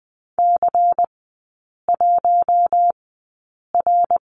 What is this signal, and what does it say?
Morse code, heard as a keyed tone, about 20 wpm, spaced as for 8 wpm, 700 Hz. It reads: C1R
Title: 093 Lesson 16e - Letter c - Three Characters 20wpm